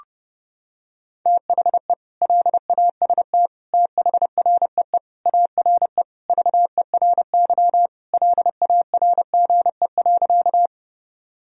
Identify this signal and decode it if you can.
Morse code, heard as a keyed tone, about 30 wpm, 700 Hz. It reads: THE LAST THREE ARE VERY LARGE.
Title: Sentences from Top 1000 Words - Review Part 1 - 30wpm